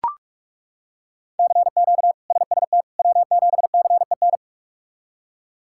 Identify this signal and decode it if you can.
Morse code, heard as a keyed tone, about 42 wpm, 700 Hz. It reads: CQ SST W7CEN
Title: K1USN SST QSO Practice - 45wpm